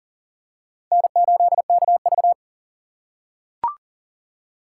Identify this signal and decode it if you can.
Morse code, heard as a keyed tone, about 40 wpm, 700 Hz. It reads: N8KV
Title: USA Call Signs - 4 Characters - 3x Character Spacing - 40wpm